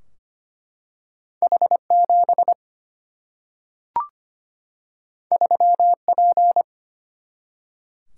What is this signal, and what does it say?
Morse code, heard as a keyed tone, about 25 wpm, 700 Hz. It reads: H7 3P
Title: Single-Letter Single-Number 25wpm